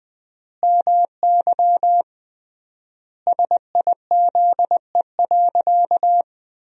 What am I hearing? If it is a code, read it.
Morse code, heard as a keyed tone, about 20 wpm, 700 Hz. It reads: MY SIZE.
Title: 3x Word-Spacing - Sentences from Top 400 Words - 20wpm